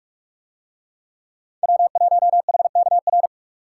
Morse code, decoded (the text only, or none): W1HKR